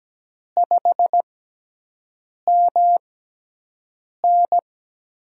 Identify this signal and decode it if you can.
Morse code, heard as a keyed tone, about 17 wpm, spaced as for 6 wpm, 700 Hz. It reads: 5MN
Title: USA Call Signs - 6x Character Spacing - 17wpm